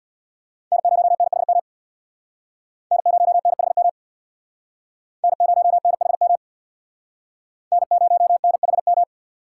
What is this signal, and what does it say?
Morse code, heard as a keyed tone, about 62 wpm, 700 Hz. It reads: N0NHM N0NHM N0NHM N0NHM
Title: Speed Racing - USA 1x3 Call Signs 50wpm